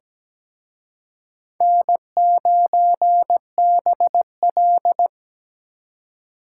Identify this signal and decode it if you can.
Morse code, heard as a keyed tone, about 20 wpm, 700 Hz. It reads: N9BL
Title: Speed Racing - USA 1x2 Call Signs 17wpm